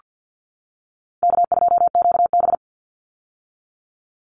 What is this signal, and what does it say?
Morse code, heard as a keyed tone, about 48 wpm, 700 Hz. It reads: K2QB
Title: Easy Call Signs - 50wpm